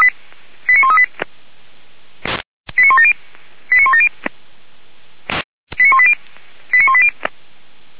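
Five-tone selective call (ZVEI), heard as quick runs of five tones, tones of about 70 ms. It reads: 89138 89170 89170 89180 89180
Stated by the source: ZVEI five-tone call received by SDR, from feuerwehr-orlamuende.org.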